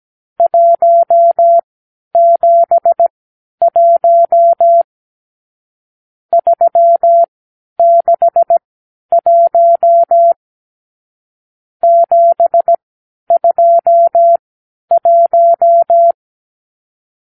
Morse code, heard as a keyed tone, about 17 wpm, spaced as for 10 wpm, 680 Hz. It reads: 171 361 721
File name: k7qo / Lesson 54 (9)